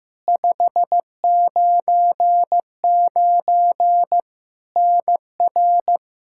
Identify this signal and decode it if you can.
Morse code, heard as a keyed tone, about 15 wpm, 700 Hz. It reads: 599 NR
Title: Practice QSO - Exchange 1 - Beginning - 15wpm